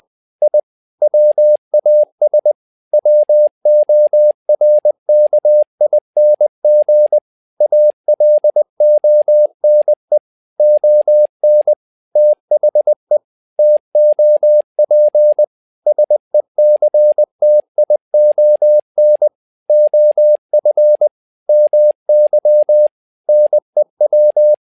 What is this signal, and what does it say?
Morse code, heard as a keyed tone, about 20 wpm, 600 Hz. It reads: I WAS WORKING ALONE ON THE TOP SECTION OF MY NEW
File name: tower trouble 20wpm